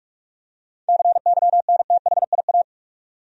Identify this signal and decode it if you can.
Morse code, heard as a keyed tone, about 47 wpm, 700 Hz. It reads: CYNTHIA
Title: USA Common Names - 45wpm